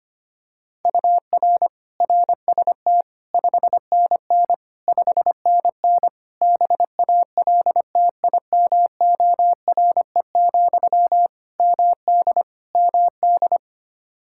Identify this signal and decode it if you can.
Morse code, heard as a keyed tone, about 25 wpm, 700 Hz. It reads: UR RST 5NN 5NN BALTIMORE, MD MD
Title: Practice QSO - Exchange 1 - Beginning - 25wpm